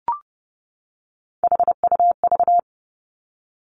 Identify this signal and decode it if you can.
Morse code, heard as a keyed tone, about 30 wpm, 700 Hz. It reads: HUV
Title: Three Challenging Letter-Number Combinations 30wpm